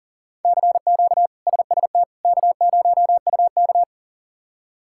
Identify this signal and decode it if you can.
Morse code, heard as a keyed tone, about 37 wpm, 700 Hz. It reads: CQ SST K0UK
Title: K1USN SST QSO Practice - 40wpm